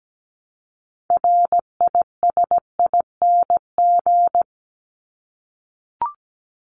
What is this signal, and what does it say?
Morse code, heard as a keyed tone, about 17 wpm, 700 Hz. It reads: RISING
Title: ING Suffix - Top 100 Words with Base Words 17wpm